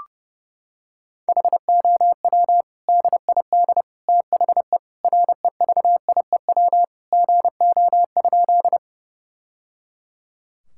Morse code, heard as a keyed tone, about 30 wpm, 700 Hz. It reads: HOW DID THE REVIEW GO?
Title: Sentences from Top 1000 Words - Review Part 6 - 30wpm